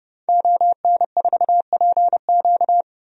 Morse code, heard as a keyed tone, about 30 wpm, 700 Hz. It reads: ON4PQ
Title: DXCC-100 Call Signs - 5 Characters - 30wpm